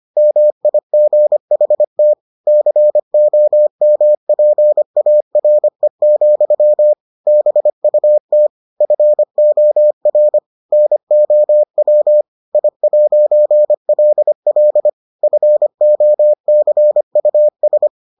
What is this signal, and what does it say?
Morse code, heard as a keyed tone, about 25 wpm, 600 Hz. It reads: MIGHT COMPARE, BUT FOR NOW I'LL FOCUS